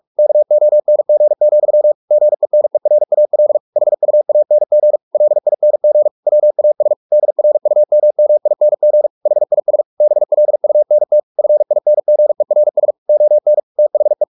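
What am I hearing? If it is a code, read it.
Morse code, heard as a keyed tone, about 44 wpm, 600 Hz. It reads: KONG, GENERAL HUANG LING WAS DRUMMING HIS BLUNT FINGERS ON THE
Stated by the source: Morse practice at 45 wpm, the novel Never by Ken Follett in English, from CW med Gnister.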